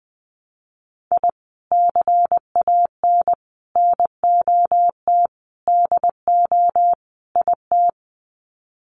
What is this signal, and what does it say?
Morse code, heard as a keyed tone, about 20 wpm, 700 Hz. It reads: I CAN NOT DO IT
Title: Sets of 5 Words Encore 20wpm